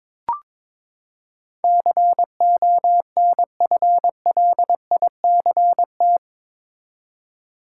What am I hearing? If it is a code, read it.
Morse code, heard as a keyed tone, about 22 wpm, 700 Hz. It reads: CONFLICT
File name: practice / 22wpm / Eight Letter Words - 22wpm